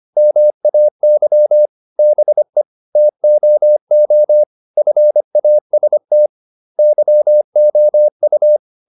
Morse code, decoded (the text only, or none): MAY BE TOO FAST YOU